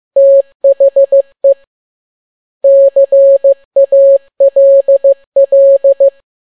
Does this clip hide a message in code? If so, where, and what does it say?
Yes: Morse code from 0.2 to 6.1 s, heard as a keyed tone, about 15 wpm, 550 Hz. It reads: THE CALL